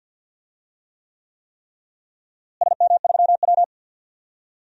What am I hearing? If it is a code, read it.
Morse code, heard as a keyed tone, about 50 wpm, 700 Hz. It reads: SM3W
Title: DXCC-100 Call Signs - 4 Characters - Repeated 3x - 50wpm